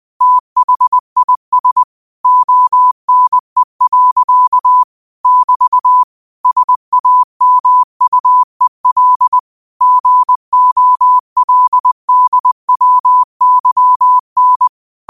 Morse code, heard as a keyed tone, about 20 wpm, 1000 Hz. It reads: THIS ONE. <BT> SAMUEL GOLDWYN